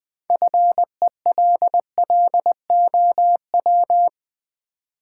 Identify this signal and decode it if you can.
Morse code, heard as a keyed tone, about 20 wpm, 700 Hz. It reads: FELLOW
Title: Sets of 5 Words Encore - New Word Drill - 20wpm